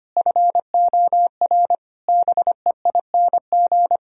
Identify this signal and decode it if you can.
Morse code, heard as a keyed tone, about 25 wpm, 700 Hz. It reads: FOR BEING
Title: Sentences from Top 1000 Words - Review Part 3 - 25wpm